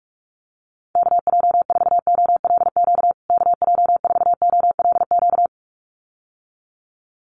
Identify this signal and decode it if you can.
Morse code, heard as a keyed tone, about 44 wpm, 700 Hz. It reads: KJ4OLQ KJ4OLQ
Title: USA Call Signs Repeated 2x 45wpm